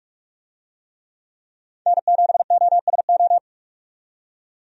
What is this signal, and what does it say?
Morse code, heard as a keyed tone, about 46 wpm, 700 Hz. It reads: N7OSO
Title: USA Call Signs - 5 Characters - 45wpm